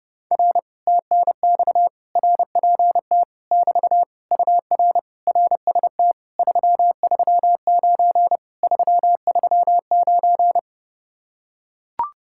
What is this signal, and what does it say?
Morse code, heard as a keyed tone, about 30 wpm, 700 Hz. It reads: R TNX RPT <BT> UR RST 339 339